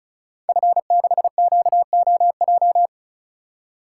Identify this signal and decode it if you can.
Morse code, heard as a keyed tone, about 36 wpm, 700 Hz. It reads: F6QOJ
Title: Code Groups - 5 Letter:Numbers - 35wpm